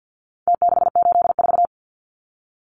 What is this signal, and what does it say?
Morse code, heard as a keyed tone, about 48 wpm, 700 Hz. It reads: T684
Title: Numbers - Contest Style - 4-Digit 50wpm